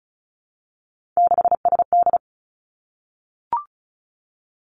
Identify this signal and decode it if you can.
Morse code, heard as a keyed tone, about 36 wpm, 700 Hz. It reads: 6SD
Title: Three Challenging Letter-Number Combinations 35wpm